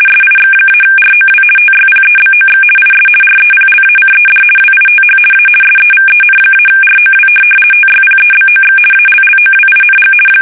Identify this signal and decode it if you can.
RTTY, about 50 baud. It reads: SIX-ROW TYPEBOX IS REPLACED WITH A SEVEN-ROW TYPEBOX ALLOWING
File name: fsk850-50B / teletype-corp